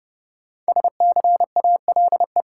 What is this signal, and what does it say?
Morse code, heard as a keyed tone, about 32 wpm, 700 Hz. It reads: SCALE
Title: Sentences from Top 1000 Words - Review Part 4 - New Word Drill - 30wpm